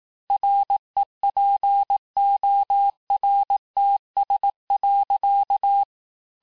Morse code, heard as a keyed tone, about 18 wpm, 800 Hz. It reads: REPORTS.
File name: cours-du-lundi files / cours-1588-FIN-de-QSO-vitesse-17MPM